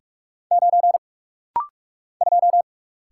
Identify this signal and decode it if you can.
Morse code, heard as a keyed tone, about 44 wpm, 700 Hz. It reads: 9 2
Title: Single Letter-Number Alt - Rapid-Fire 45wpm